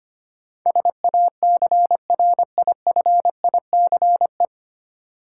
Morse code, heard as a keyed tone, about 25 wpm, 700 Hz. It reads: SACRIFICE
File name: practice / 25wpm / 9 Letter Words - First 3 Letters Given - 25wpm